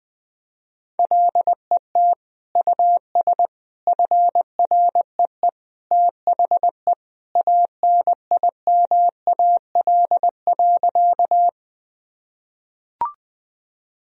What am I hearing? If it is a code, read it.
Morse code, heard as a keyed tone, about 21 wpm, 700 Hz. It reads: LET US FREE THE ANIMAL.
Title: Sentences from Top 1000 Words - Review Part 1 - 20wpm